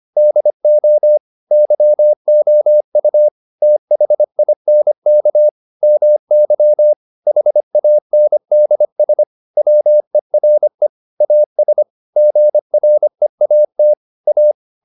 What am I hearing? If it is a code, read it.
Morse code, heard as a keyed tone, about 25 wpm, 600 Hz. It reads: DO YOU THINK MY HANDS WERE AS GREAT A